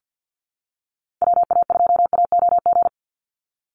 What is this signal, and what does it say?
Morse code, heard as a keyed tone, about 50 wpm, 700 Hz. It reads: WA2AOG
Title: Call Signs - 50wpm